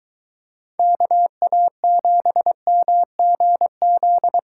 Morse code, heard as a keyed tone, about 21 wpm, 700 Hz. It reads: KA7MGZ